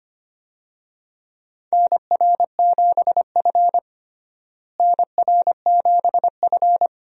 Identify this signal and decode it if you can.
Morse code, heard as a keyed tone, about 25 wpm, 700 Hz. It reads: NR7F NR7F
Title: USA Call Signs - 4 Characters - Repeated 2x - 25wpm